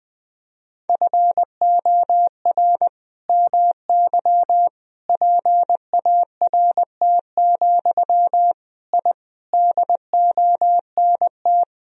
Morse code, heard as a keyed tone, about 20 wpm, 700 Hz. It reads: FOR MY PART, I DONT
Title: Sentences from Top 1000 Words - Review Part 1 - 20wpm